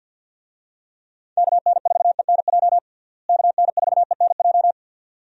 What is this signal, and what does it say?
Morse code, heard as a keyed tone, about 47 wpm, 700 Hz. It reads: KN4ENJ KN4ENJ
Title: USA Call Signs - 6 Characters - Repeated 2x - 50wpm